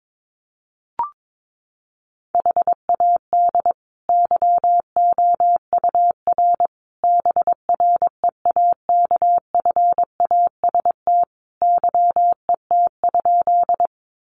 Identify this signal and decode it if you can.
Morse code, heard as a keyed tone, about 22 wpm, 700 Hz. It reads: HAD YOUR BREAKFAST YET?
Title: Sentences from Top 1500 Words 22wpm